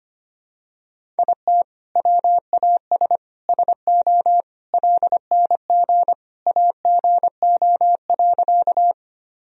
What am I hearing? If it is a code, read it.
Morse code, heard as a keyed tone, about 25 wpm, 700 Hz. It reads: IT WAS SO LNG AGO.
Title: Sentences from Top 500 Words and Abbreviations 25wpm